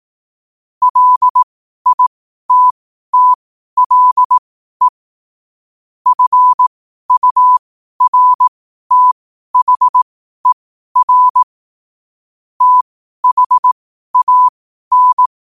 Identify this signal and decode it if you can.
Morse code, heard as a keyed tone, about 18 wpm, spaced as for 12 wpm, 1000 Hz. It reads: LITTLE FURTHER THAN
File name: 12wpm